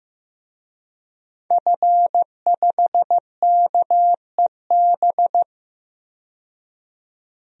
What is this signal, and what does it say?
Morse code, heard as a keyed tone, about 15 wpm, 700 Hz. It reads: F5KEB
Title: DXCC-100 Call Signs - 5 Characters - 15wpm